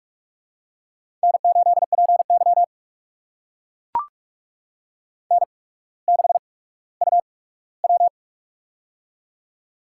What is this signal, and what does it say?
Morse code, heard as a keyed tone, about 42 wpm, spaced as for 19 wpm, 700 Hz. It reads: N8PY N6UW
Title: USA Call Signs - 4 Characters - 8x Character Spacing - 45wpm